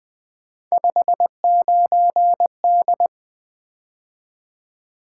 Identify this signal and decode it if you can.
Morse code, heard as a keyed tone, about 20 wpm, 700 Hz. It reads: 59D
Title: Three Letter-Number Combinations - 20wpm